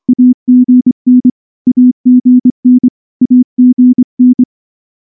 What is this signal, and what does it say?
Morse code, heard as a keyed tone, about 24 wpm, 250 Hz. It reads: AGN AGN AGN
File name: morse / morse-training-25WPM